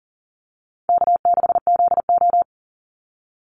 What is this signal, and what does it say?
Morse code, heard as a keyed tone, about 40 wpm, 700 Hz. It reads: K6ZO
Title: Easy Call Signs - 40wpm